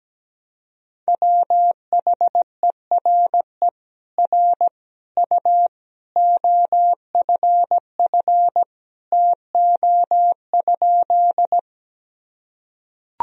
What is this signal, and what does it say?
Morse code, heard as a keyed tone, about 17 wpm, 700 Hz. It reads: WHERE R U OFF TO?
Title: Sentences from Top 200 Words and Abbreviations 17wpm